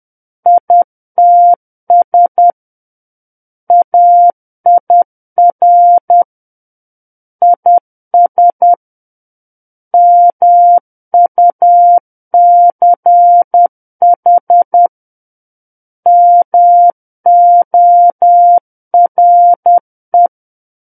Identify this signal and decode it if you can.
Morse code, heard as a keyed tone, about 10 wpm, 700 Hz. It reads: ITS AIR IS MUCH MORE
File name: Book One / 01-Chapter 1 - 10 WPM